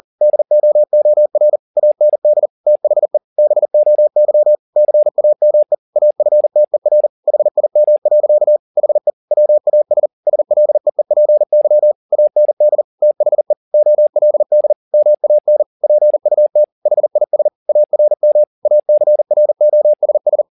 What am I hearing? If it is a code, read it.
Morse code, heard as a keyed tone, about 40 wpm, 600 Hz. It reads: DOOR AND THE BOY CAME AFTER HIM. HE WAS SLEEPY AND THE OLD MAN PUT HIS ARM ACROSS